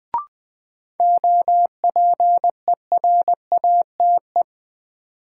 Morse code, heard as a keyed tone, about 20 wpm, 700 Hz. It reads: OPERATE
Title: Sentences from Top 1500 Words - New Word Drill - 20wpm